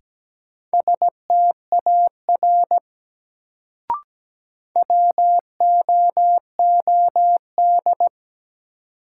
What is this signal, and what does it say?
Morse code, heard as a keyed tone, about 17 wpm, 700 Hz. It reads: STAR WOOD
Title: Sentences from Top 1000 Words - Review Part 2 - New Word Drill - 17wpm